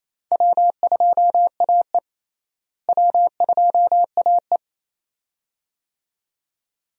Morse code, heard as a keyed tone, about 28 wpm, 700 Hz. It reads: W2AE W2AE
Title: USA Call Signs - 4 Characters - Repeated 2x - 28wpm